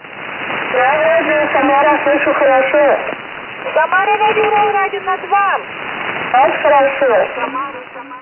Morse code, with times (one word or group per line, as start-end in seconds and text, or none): none